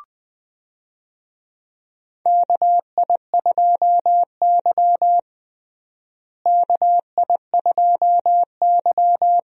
Morse code, heard as a keyed tone, about 21 wpm, 700 Hz. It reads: KI2Y KI2Y
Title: USA Call Signs - 4 Characters - Repeated 2x - 20wpm